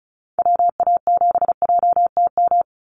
Morse code, heard as a keyed tone, about 36 wpm, 700 Hz. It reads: WA7JTM
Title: NA SOTA Activator Callsigns 35wpm